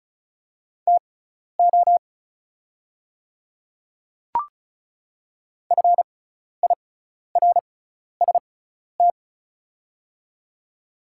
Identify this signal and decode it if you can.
Morse code, heard as a keyed tone, about 34 wpm, spaced as for 12 wpm, 700 Hz. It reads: TO FIRST
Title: 6x Character Spacing - Top 100 Words - 35wpm